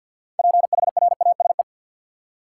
Morse code, heard as a keyed tone, about 52 wpm, 700 Hz. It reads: PHRASE
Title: Six Letter Words - 50wpm